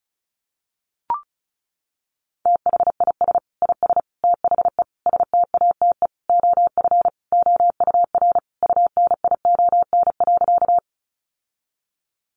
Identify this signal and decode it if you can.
Morse code, heard as a keyed tone, about 34 wpm, 700 Hz. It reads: THIS IS THE STATE OF OUR UNION.
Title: Sentences from Top 1400 Words 35wpm